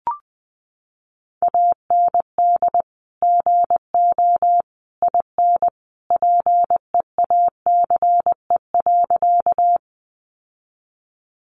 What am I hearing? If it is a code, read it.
Morse code, heard as a keyed tone, about 20 wpm, 700 Hz. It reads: AND GO IN PEACE.
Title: Sentences from Top 1500 Words 20wpm